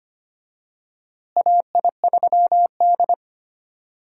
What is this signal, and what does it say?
Morse code, heard as a keyed tone, about 25 wpm, 700 Hz. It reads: AI3D
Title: USA Call Signs - 4 Characters - Repeated 3x - 25wpm